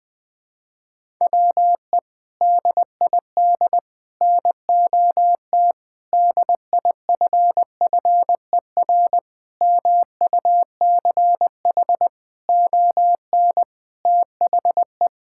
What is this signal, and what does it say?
Morse code, heard as a keyed tone, about 20 wpm, 700 Hz. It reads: WE DID NOT DIFFER MUCH ON THE
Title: Sentences from Top 1000 Words - Review Part 6 - 20wpm